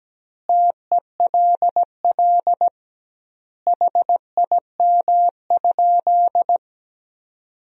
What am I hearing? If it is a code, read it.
Morse code, heard as a keyed tone, about 17 wpm, 700 Hz. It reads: TELL HIM?
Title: 2x Word-Spacing - Sentences from Top 200 Words - 17wpm